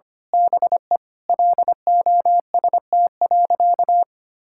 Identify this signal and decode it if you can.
Morse code, heard as a keyed tone, about 25 wpm, 700 Hz. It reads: BE LOST.
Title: Sentences from Top 1600 Words 25wpm